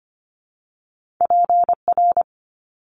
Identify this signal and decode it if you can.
Morse code, heard as a keyed tone, about 25 wpm, 700 Hz. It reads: PR
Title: Top Two Letter Combinations in Words 25wpm